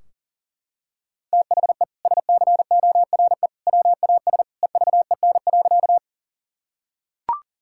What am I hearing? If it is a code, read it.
Morse code, heard as a keyed tone, about 40 wpm, 700 Hz. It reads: THE SCORE WAS EVEN.
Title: Sentences from Top 1600 Words 40wpm